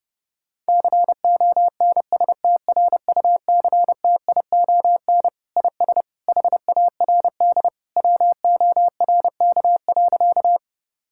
Morse code, heard as a keyed tone, about 30 wpm, 700 Hz. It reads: CONSTRUCTION IS HARD WORK.
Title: Sentences from Top 1600 Words 30wpm